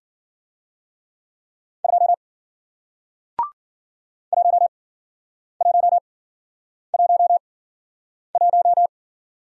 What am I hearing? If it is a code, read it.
Morse code, heard as a keyed tone, about 54 wpm, spaced as for 9 wpm, 700 Hz. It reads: 2 1111